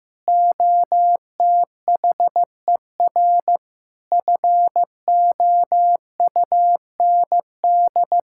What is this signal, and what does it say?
Morse code, heard as a keyed tone, about 16 wpm, 700 Hz. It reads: OTHER FOUND
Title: Sentences from Top 1600 Words 15wpm